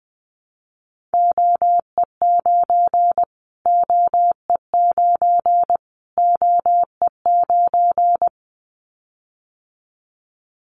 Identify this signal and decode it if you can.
Morse code, heard as a keyed tone, about 20 wpm, 700 Hz. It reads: OE9 OE9 OE9
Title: Callsign Prefixes for Top 25 DXCC - Repeated 3x - 20wpm